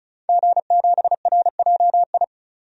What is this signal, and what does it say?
Morse code, heard as a keyed tone, about 34 wpm, 700 Hz. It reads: G7RJI
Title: DXCC-100 Call Signs - 5 Characters - 35wpm